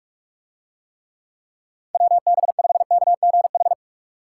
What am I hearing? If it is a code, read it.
Morse code, heard as a keyed tone, about 47 wpm, 700 Hz. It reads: WB5KGH